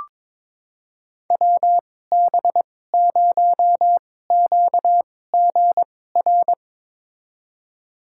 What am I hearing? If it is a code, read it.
Morse code, heard as a keyed tone, about 22 wpm, spaced as for 16 wpm, 700 Hz. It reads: WB0QGR
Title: USA Call Signs - 6 Characters - 2x Character Spacing - 22wpm